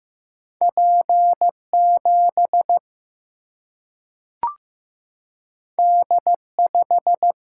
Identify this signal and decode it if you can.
Morse code, heard as a keyed tone, about 15 wpm, 700 Hz. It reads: P 7 D 5
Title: Single-Letter Single-Number 15wpm